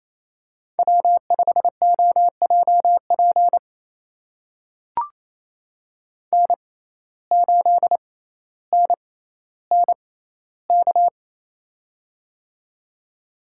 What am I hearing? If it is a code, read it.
Morse code, heard as a keyed tone, about 28 wpm, spaced as for 14 wpm, 700 Hz. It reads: W5OJP N8NNK
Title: USA Call Signs - 5 Characters - 6x Character Spacing - 28wpm